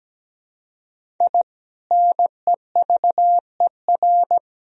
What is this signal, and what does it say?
Morse code, heard as a keyed tone, about 17 wpm, 700 Hz. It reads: I NEVER
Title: Sentences from Top 1000 Words - Review Part 1 - 17wpm